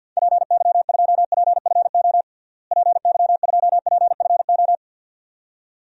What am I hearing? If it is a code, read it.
Morse code, heard as a keyed tone, about 53 wpm, 700 Hz. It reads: PY2PFO PY2PFO
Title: DXCC-100 Call Signs - 6 Characters - Repeated 2x - 50wpm